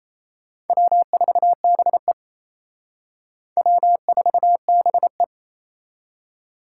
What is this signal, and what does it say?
Morse code, heard as a keyed tone, about 34 wpm, 700 Hz. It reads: W4BE W4BE
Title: Speed Racing - USA 1x2 Call Signs 28wpm